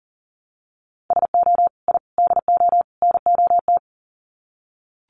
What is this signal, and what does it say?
Morse code, heard as a keyed tone, about 37 wpm, 700 Hz. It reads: SO I DO NOT
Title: Sets of 4 Words 40wpm